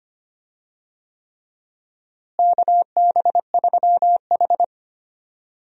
Morse code, heard as a keyed tone, about 25 wpm, 700 Hz. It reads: KB3H
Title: USA Call Signs - 4 Characters - Repeated 3x - 25wpm